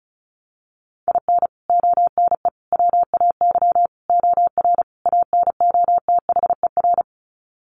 Morse code, heard as a keyed tone, about 34 wpm, 700 Hz. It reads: IN ONE WAY OR ANOTHER
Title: Sets of 5 Words Encore 35wpm